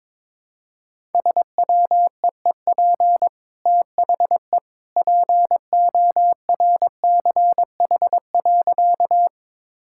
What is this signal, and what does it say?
Morse code, heard as a keyed tone, about 22 wpm, 690 Hz. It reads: SWEEP THE PORCH.
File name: practice / 22wpm / Easy Sentences with Hints - 22wpm